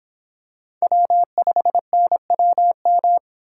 Morse code, heard as a keyed tone, about 26 wpm, 700 Hz. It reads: W5NWM